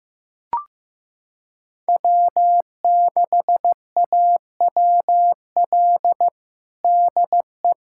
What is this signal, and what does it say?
Morse code, heard as a keyed tone, about 15 wpm, 700 Hz. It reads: W6AWL DE
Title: USA Callsign DE Callsign - 5 Characters - 15wpm